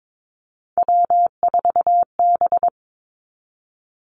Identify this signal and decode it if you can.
Morse code, heard as a keyed tone, about 22 wpm, 700 Hz. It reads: W4B
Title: Three Challenging Letter-Number Combinations 22wpm